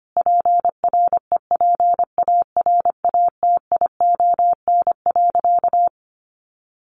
Morse code, heard as a keyed tone, about 25 wpm, 700 Hz. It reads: PREPARATION.
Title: Sentences from Top 1400 Words 25wpm